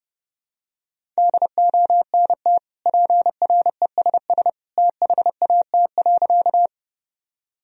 Morse code, heard as a keyed tone, about 32 wpm, 700 Hz. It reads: DONT PRESS THAT.
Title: Sentences from Top 1000 Words - Review Part 1 - 30wpm